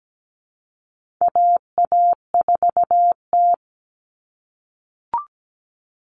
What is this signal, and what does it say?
Morse code, heard as a keyed tone, about 17 wpm, 700 Hz. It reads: AA4T
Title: All Cut Numbers - 4-Digit 17wpm